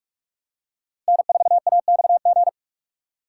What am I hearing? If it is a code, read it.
Morse code, heard as a keyed tone, about 46 wpm, 700 Hz. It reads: N4AXG
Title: USA Call Signs - 5 Characters - 3x Character Spacing - 45wpm